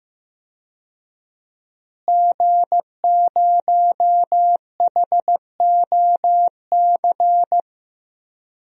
Morse code, heard as a keyed tone, about 15 wpm, 700 Hz. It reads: G0HOC
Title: DXCC-100 Call Signs - 5 Characters - Repeated 3x - 15wpm